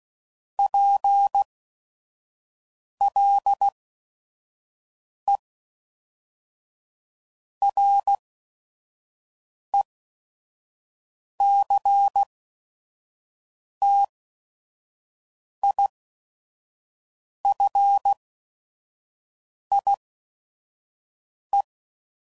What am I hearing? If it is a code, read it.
Morse code, heard as a keyed tone, about 16 wpm, spaced as for 5 wpm, 800 Hz. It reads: PLE RECTIFIE
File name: CW Code Groups @5WPM / Technical Subjects / SwPwrSup